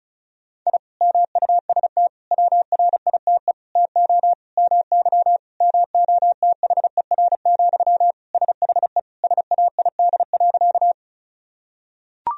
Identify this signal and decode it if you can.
Morse code, heard as a keyed tone, about 36 wpm, 700 Hz. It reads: I MUST WRITE TO MY MOTHER, SHE SAID.
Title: Sentences from Top 1000 Words - Review Part 1 - 35wpm